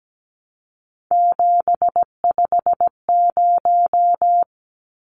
Morse code, heard as a keyed tone, about 17 wpm, 700 Hz. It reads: 750
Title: Numbers - 3-Digit 17wpm